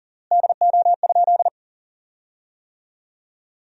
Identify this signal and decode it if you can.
Morse code, heard as a keyed tone, about 40 wpm, 700 Hz. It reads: DO?
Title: 3x Word-Spacing - Sentences from Top 200 Words - 40wpm